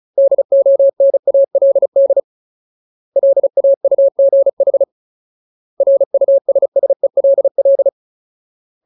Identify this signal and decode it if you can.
Morse code, heard as a keyed tone, about 34 wpm, 550 Hz. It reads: DONALD LAUGH RUSSELL
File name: Ultimate Word List / 35WPM Ultimate